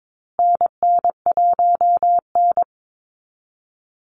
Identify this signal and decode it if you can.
Morse code, heard as a keyed tone, about 22 wpm, 700 Hz. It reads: NN1N
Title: Call Signs Repeated 3x - 22wpm